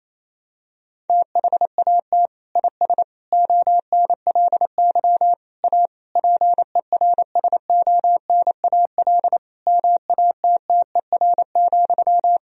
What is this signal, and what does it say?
Morse code, heard as a keyed tone, about 28 wpm, 700 Hz. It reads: THAT IS ONLY A PERSONAL MATTER,
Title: Sentences from Top 1000 Words - Review Part 4 - 28wpm